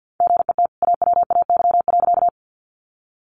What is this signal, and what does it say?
Morse code, heard as a keyed tone, about 50 wpm, 700 Hz. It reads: GET AWAY.
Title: Sentences from Top 600 Words 50wpm